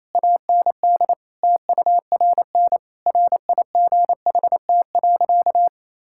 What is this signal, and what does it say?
Morse code, heard as a keyed tone, about 28 wpm, 700 Hz. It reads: AND TURN RIGHT.
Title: Sentences from Top 1000 Words - Review Part 5 - 28wpm